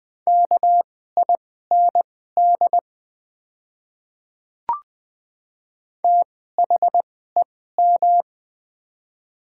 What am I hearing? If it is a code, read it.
Morse code, heard as a keyed tone, about 20 wpm, spaced as for 14 wpm, 700 Hz. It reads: KIND THEM
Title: 2x Character Spacing - Top 200 Words - 20wpm